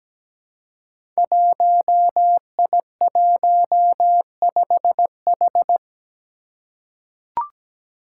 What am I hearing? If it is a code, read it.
Morse code, heard as a keyed tone, about 17 wpm, 700 Hz. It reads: 1I15H
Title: Five Challenging Letter-Number Combinations 17wpm